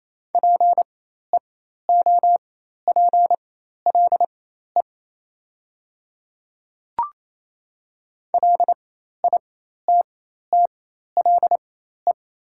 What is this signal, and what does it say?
Morse code, heard as a keyed tone, about 28 wpm, spaced as for 13 wpm, 700 Hz. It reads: PEOPLE LITTLE
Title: 4x Character Spacing - Top 200 Words - 28wpm